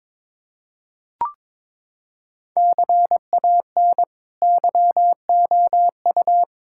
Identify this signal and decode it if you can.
Morse code, heard as a keyed tone, about 21 wpm, 700 Hz. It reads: CAN YOU
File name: practice / 22wpm / Sets of 8 Words - 22wpm